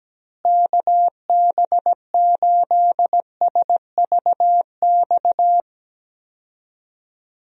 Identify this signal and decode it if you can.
Morse code, heard as a keyed tone, about 17 wpm, 700 Hz. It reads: KB8SVX
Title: USA Call Signs - 6 Characters - Repeated 2x - 17wpm